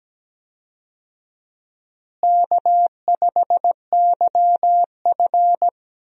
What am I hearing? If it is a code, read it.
Morse code, heard as a keyed tone, about 17 wpm, 700 Hz. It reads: K5YF
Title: USA Call Signs - 4 Characters - Repeated 2x - 17wpm